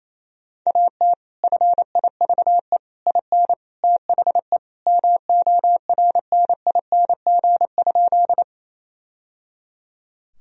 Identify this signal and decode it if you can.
Morse code, heard as a keyed tone, about 28 wpm, 700 Hz. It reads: AT FIVE IN THE MORNING?
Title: Sentences from Top 1000 Words - Review Part 2 - 28wpm